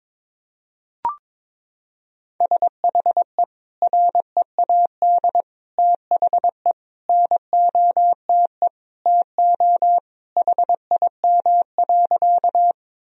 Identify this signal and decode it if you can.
Morse code, heard as a keyed tone, about 22 wpm, 700 Hz. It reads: SHE READ THE NOTE TO HIM.
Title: Sentences from Top 1000 Words - Review Part 2 - 22wpm